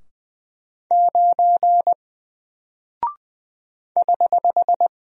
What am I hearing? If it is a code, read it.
Morse code, heard as a keyed tone, about 20 wpm, 700 Hz. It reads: 9 <HH>
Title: Single Character (Letter:Number:Punctuation:Prosign) 20wpm